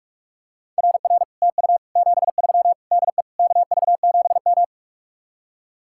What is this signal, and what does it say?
Morse code, heard as a keyed tone, about 44 wpm, 700 Hz. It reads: RR TU 73 DE KV7M